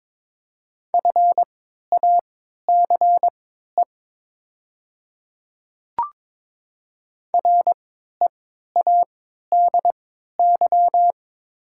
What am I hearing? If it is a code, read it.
Morse code, heard as a keyed tone, about 22 wpm, spaced as for 12 wpm, 700 Hz. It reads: FACE READY
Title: 3x Character Spacing - Top 500 Words - 22wpm